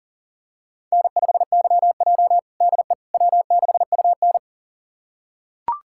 Morse code, heard as a keyed tone, about 40 wpm, 700 Hz. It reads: N5YJ DE W6UN
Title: USA Callsign DE Callsign - 4 Characters - 40wpm